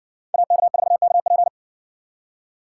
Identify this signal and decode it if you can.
Morse code, heard as a keyed tone, about 56 wpm, 700 Hz. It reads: AK4KP